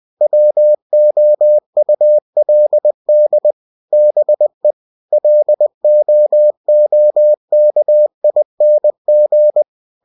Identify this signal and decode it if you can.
Morse code, heard as a keyed tone, about 20 wpm, 600 Hz. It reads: WOULD BE LOOKING
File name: cw-books / winnie-the-pooh / winnie-20wpm-page-0035